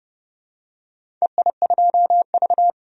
Morse code, heard as a keyed tone, about 31 wpm, 700 Hz. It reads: EI2V
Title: DXCC-100 Call Signs - 4 Characters - 30wpm